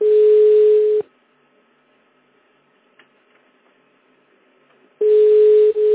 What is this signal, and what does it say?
Ringback tone (Europe, 425 Hz).